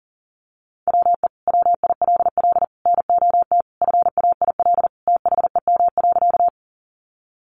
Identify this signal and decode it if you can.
Morse code, heard as a keyed tone, about 40 wpm, 700 Hz. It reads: WE WILL NOT FAIL THEM.
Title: Sentences from Top 1100 Words 40wpm